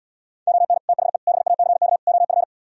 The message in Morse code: ON THE BACK OF

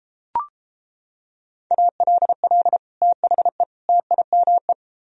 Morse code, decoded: ALL THE TIME

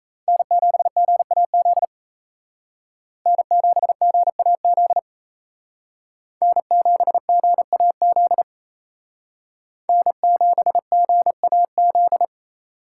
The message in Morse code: N7GAZ N7GAZ N7GAZ N7GAZ

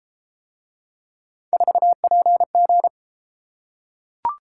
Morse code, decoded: 4PG

